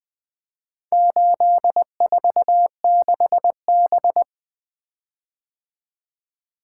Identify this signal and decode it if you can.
Morse code, heard as a keyed tone, about 20 wpm, 700 Hz. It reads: 846B
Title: Four Letter-Number Combinations - 20wpm